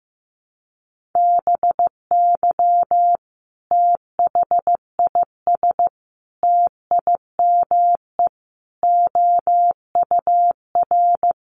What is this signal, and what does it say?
Morse code, heard as a keyed tone, about 15 wpm, 700 Hz. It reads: BY THIS TIME OUR
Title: Sentences from Top 1400 Words 15wpm